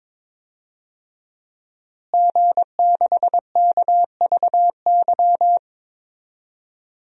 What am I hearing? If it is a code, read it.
Morse code, heard as a keyed tone, about 22 wpm, 700 Hz. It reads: G6KVY
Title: DXCC-100 Call Signs - 5 Characters - Repeated 2x - 22wpm